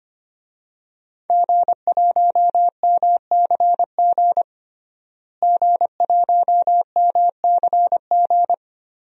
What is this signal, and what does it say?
Morse code, heard as a keyed tone, about 25 wpm, 700 Hz. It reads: G1MCG G1MCG
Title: DXCC-100 Call Signs - 5 Characters - Repeated 3x - 25wpm